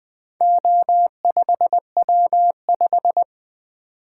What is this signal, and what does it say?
Morse code, heard as a keyed tone, about 20 wpm, 700 Hz. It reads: O5W5
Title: Four Letter-Number Combinations - 20wpm